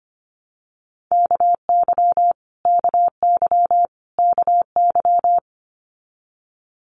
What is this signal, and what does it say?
Morse code, heard as a keyed tone, about 25 wpm, 700 Hz. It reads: KY KY KY
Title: Multi-letter Phonemes Repeated 3x 25wpm